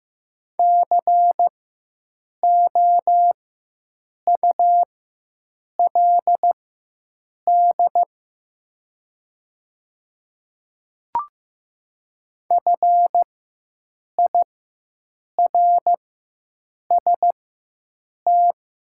Morse code, heard as a keyed tone, about 15 wpm, spaced as for 7 wpm, 700 Hz. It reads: COULD FIRST
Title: 4x Character Spacing - Top 200 Words - 15wpm